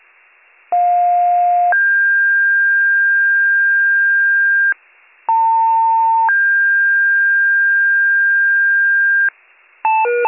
Two-tone page, heard as a one-second tone, then a three-second tone, about 710 then 1600 Hz, about 900 then 1600 Hz. It is heard from 0.7 to 9.3 s.